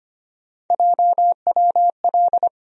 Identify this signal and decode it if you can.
Morse code, heard as a keyed tone, about 25 wpm, 700 Hz. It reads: JWL
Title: USA Call Sign Postfixes - 3 Letters 25wpm